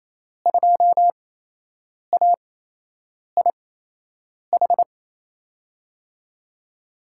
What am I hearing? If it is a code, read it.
Morse code, heard as a keyed tone, about 27 wpm, spaced as for 8 wpm, 700 Hz. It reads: 2AIH
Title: USA Call Signs - 8x Character Spacing - 28wpm